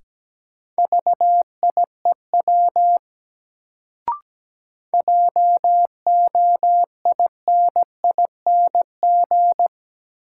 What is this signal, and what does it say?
Morse code, heard as a keyed tone, about 17 wpm, 700 Hz. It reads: VIEW JOINING
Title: Sets of 5 Words - New Word Drill - 17wpm